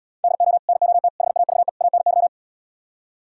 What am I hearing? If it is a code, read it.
Morse code, heard as a keyed tone, about 75 wpm, 700 Hz. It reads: DO NOT HAVE ANY